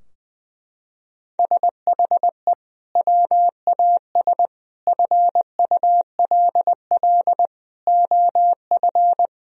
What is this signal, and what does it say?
Morse code, heard as a keyed tone, about 21 wpm, 700 Hz. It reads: SHE WAS FULL OF
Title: Sentences from Top 1000 Words - Review Part 3 - 20wpm